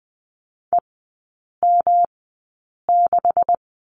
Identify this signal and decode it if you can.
Morse code, heard as a keyed tone, about 20 wpm, spaced as for 8 wpm, 700 Hz. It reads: EM6
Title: 184 Lesson 30e - Letter b - Three Characters 20wpm